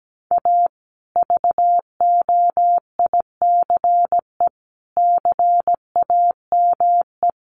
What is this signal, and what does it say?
Morse code, heard as a keyed tone, about 17 wpm, 700 Hz. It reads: A VOICE CAME
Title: Sentences from Top 1200 Words 17wpm